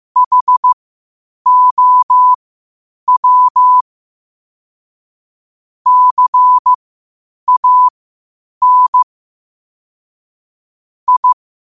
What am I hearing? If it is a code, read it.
Morse code, heard as a keyed tone, about 15 wpm, spaced as for 8 wpm, 1000 Hz. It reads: HOW CAN I
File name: qotdfiles / 8wpm